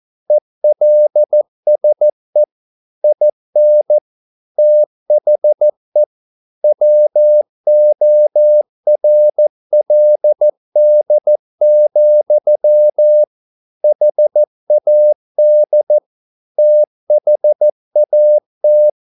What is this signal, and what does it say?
Morse code, heard as a keyed tone, about 14 wpm, 600 Hz. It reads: ELSE IN THE WORLD, HAD THAT